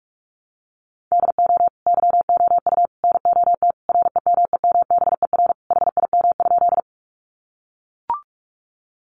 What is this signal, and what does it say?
Morse code, heard as a keyed tone, about 44 wpm, 700 Hz. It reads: DO YOU NOT REMEMBER HIM?
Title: Sentences from Top 600 Words 45wpm